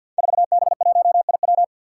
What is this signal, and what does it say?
Morse code, heard as a keyed tone, about 52 wpm, 700 Hz. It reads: 4B1IW
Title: Five Challenging Letter-Number Combinations 50wpm